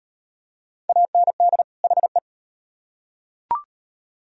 Morse code, heard as a keyed tone, about 32 wpm, 700 Hz. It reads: AND HE